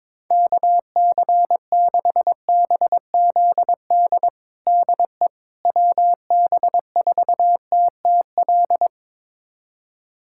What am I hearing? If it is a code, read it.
Morse code, heard as a keyed tone, about 21 wpm, 700 Hz. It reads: KC6BZD DE WB4TTL
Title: USA Callsign DE Callsign - 6 Characters - 22wpm